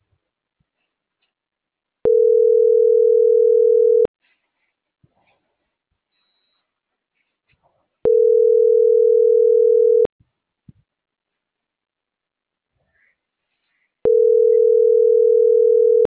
Ringback tone (North America), 440 + 480 Hz.